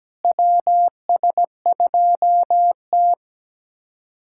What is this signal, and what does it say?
Morse code, heard as a keyed tone, about 17 wpm, 700 Hz. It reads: WS2T
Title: USA Callsign DE Callsign - 4 Characters - 17wpm